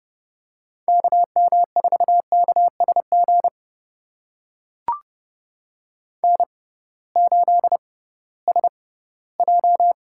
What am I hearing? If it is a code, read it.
Morse code, heard as a keyed tone, about 32 wpm, spaced as for 18 wpm, 700 Hz. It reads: KM4KSG N8SJ
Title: USA Call Signs - 6x Character Spacing - 30wpm